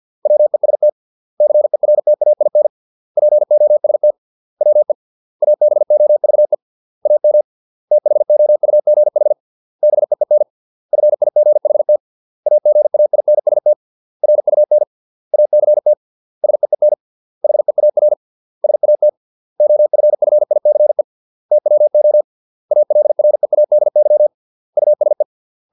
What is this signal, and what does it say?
Morse code, heard as a keyed tone, about 50 wpm, 600 Hz. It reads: WEST CERTAIN POST WE ABOVE AM THOUGH BEEN FIGHT AGAINST RUN ACT SEEN HEAR SAT OFFICE TWO ALREADY USE